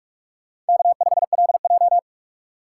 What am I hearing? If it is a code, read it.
Morse code, heard as a keyed tone, about 58 wpm, 700 Hz. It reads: K5LJ